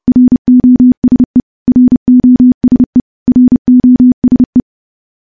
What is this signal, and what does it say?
Morse code, heard as a keyed tone, about 30 wpm, 250 Hz. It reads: ROSE ROSE ROSE
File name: morse / morse-training-30WPM